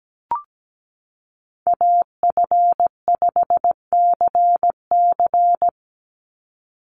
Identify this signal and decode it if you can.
Morse code, heard as a keyed tone, about 17 wpm, 700 Hz. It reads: AF5CC
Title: Call Signs Repeated 3x - 17wpm